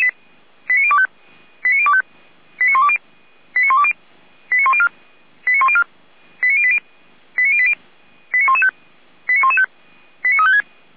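Five-tone selective call (ZVEI), heard as quick runs of five tones, tones of about 70 ms. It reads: 89025 89025 89120 89120 89104 89104 89080 89080 89115 89115 89356
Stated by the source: Five-tone ZVEI selective call received by SDR.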